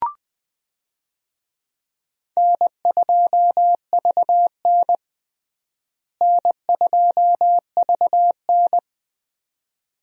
Morse code, heard as a keyed tone, about 21 wpm, 700 Hz. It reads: N2VN N2VN
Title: USA Call Signs - 4 Characters - Repeated 2x - 20wpm